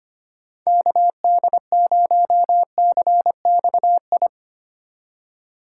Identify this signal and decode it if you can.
Morse code, heard as a keyed tone, about 25 wpm, 700 Hz. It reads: KD0CXI
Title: USA Call Signs - 6 Characters - 2x Character Spacing - 25wpm